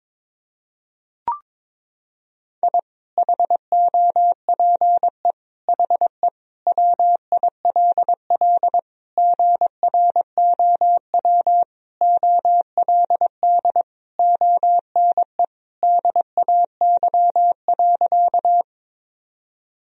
Morse code, heard as a keyed tone, about 22 wpm, 700 Hz. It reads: I HOPE HE WILL GROW OLD ONE DAY.
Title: Sentences from Top 1000 Words - Review Part 1 - 22wpm